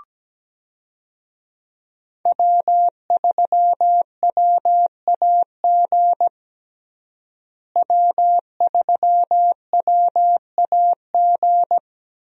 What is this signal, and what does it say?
Morse code, heard as a keyed tone, about 17 wpm, 700 Hz. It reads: W3WAG W3WAG
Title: USA Call Signs - 5 Characters - Repeated 2x - 17wpm